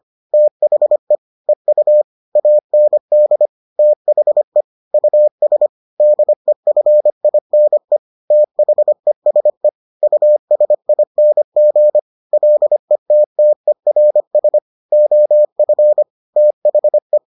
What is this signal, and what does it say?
Morse code, heard as a keyed tone, about 25 wpm, 600 Hz. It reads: THE EU AND THE US DEFINE THESE USING LETTERS OF THE